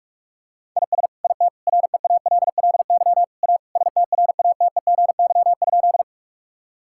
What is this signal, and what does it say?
Morse code, heard as a keyed tone, about 46 wpm, 700 Hz. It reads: IS IT REALLY A STRATEGY?
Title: Sentences from Top 1000 Words - Review Part 5 - 45wpm